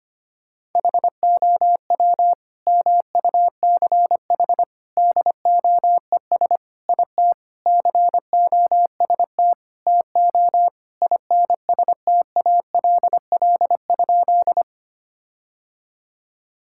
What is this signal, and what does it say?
Morse code, heard as a keyed tone, about 25 wpm, 700 Hz. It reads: HOW MUCH DOES IT COST TO INSTALL?
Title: Sentences from Top 1600 Words 25wpm